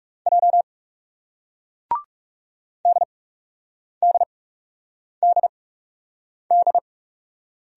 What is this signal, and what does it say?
Morse code, heard as a keyed tone, about 38 wpm, spaced as for 8 wpm, 700 Hz. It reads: J DDDD